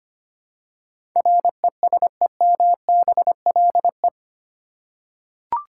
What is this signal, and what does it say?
Morse code, heard as a keyed tone, about 25 wpm, 700 Hz. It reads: RESEMBLE
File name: 8 Letter Words - First 3 Letters Given - 25wpm